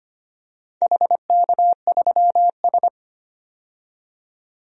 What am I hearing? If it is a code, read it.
Morse code, heard as a keyed tone, about 25 wpm, 700 Hz. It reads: HK3S
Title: DXCC-100 Call Signs - 4 Characters - 25wpm